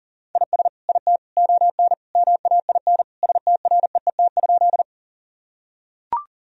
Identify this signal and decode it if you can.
Morse code, heard as a keyed tone, about 40 wpm, 700 Hz. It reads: IS IT ON MAIN STREET?
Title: Sentences from Top 1000 Words - Review Part 2 - 40wpm